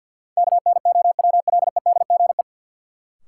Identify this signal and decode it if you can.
Morse code, heard as a keyed tone, about 47 wpm, 700 Hz. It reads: KNOWLEDGE
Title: Sentences from Top 600 Words - New Word Drill - 50wpm